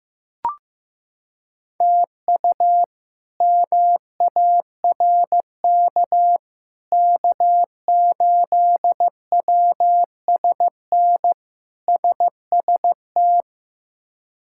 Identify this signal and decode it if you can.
Morse code, heard as a keyed tone, about 15 wpm, 700 Hz. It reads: TU MARK K8WSN SST
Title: K1USN SST QSO Practice - 15wpm